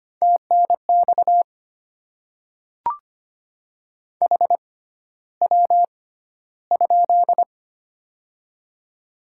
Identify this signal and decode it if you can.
Morse code, heard as a keyed tone, about 25 wpm, spaced as for 13 wpm, 700 Hz. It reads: TNX HW?
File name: Top 20 QSO Elements - 6x Character Spacing - 25wpm